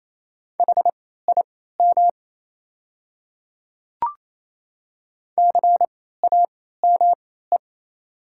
Morse code, heard as a keyed tone, about 27 wpm, spaced as for 16 wpm, 700 Hz. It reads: HIM CAME
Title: 3x Character Spacing - Top 200 Words - 28wpm